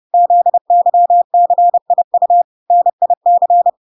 Morse code, heard as a keyed tone, about 30 wpm, 700 Hz. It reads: ZYCIU NIC